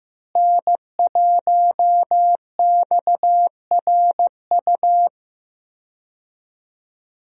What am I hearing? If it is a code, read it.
Morse code, heard as a keyed tone, about 15 wpm, 700 Hz. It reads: N1XRU
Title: USA Call Signs - 5 Characters - Repeated 2x - 15wpm